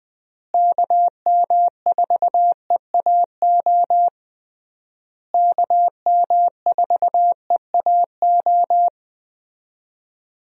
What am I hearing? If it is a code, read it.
Morse code, heard as a keyed tone, about 21 wpm, 700 Hz. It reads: KM4EAO KM4EAO
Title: USA Call Signs - 6 Characters - Repeated 3x - 20wpm